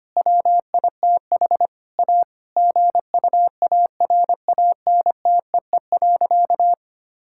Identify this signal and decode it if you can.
Morse code, heard as a keyed tone, about 25 wpm, 700 Hz. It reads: WITH A GUARANTEE.
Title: Sentences from Top 1600 Words 25wpm